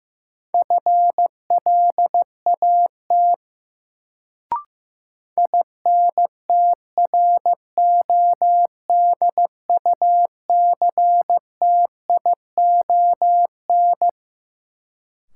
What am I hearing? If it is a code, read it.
Morse code, heard as a keyed tone, about 15 wpm, 700 Hz. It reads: FLAT INTRODUCTION
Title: Sentences from Top 1000 Words - New Word Drill - 15wpm